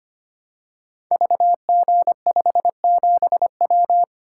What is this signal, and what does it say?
Morse code, heard as a keyed tone, about 25 wpm, 700 Hz. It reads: VG57W
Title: Code Groups - 5 Letter:Numbers - 25wpm